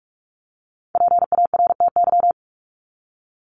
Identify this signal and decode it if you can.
Morse code, heard as a keyed tone, about 44 wpm, 700 Hz. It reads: PARTY
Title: ES:S Suffix - Top 100 Words with Base Words 45wpm